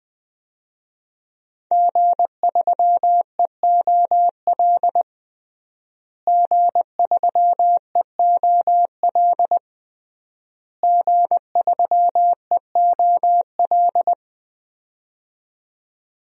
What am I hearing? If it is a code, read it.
Morse code, heard as a keyed tone, about 20 wpm, 700 Hz. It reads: G3EOL G3EOL G3EOL
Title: DXCC-100 Call Signs - 5 Characters - Repeated 3x - 20wpm